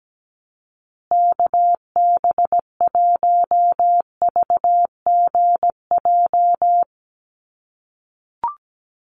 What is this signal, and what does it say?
Morse code, heard as a keyed tone, about 17 wpm, 700 Hz. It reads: KB1VGJ
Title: USA Call Signs Repeated 2x 17wpm